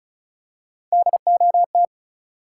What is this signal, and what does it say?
Morse code, heard as a keyed tone, about 36 wpm, 700 Hz. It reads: DOT
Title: Three Letter Words - 35wpm